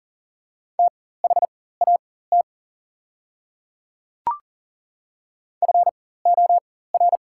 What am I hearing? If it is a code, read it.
Morse code, heard as a keyed tone, about 40 wpm, spaced as for 19 wpm, 700 Hz. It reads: THAT FOR
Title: 4x Character Spacing - Top 100 Words - 40wpm